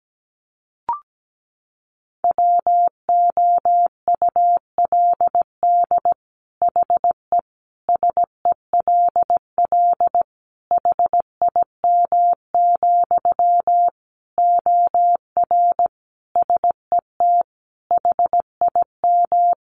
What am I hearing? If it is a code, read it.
Morse code, heard as a keyed tone, about 17 wpm, 700 Hz. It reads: WOULD HE SELL HIM, OR SET HIM
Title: Sentences from Top 1100 Words 17wpm